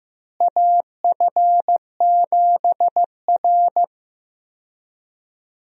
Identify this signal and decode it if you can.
Morse code, heard as a keyed tone, about 15 wpm, 700 Hz. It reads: AF7R
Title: USA Call Signs - 4 Characters - Repeated 2x - 15wpm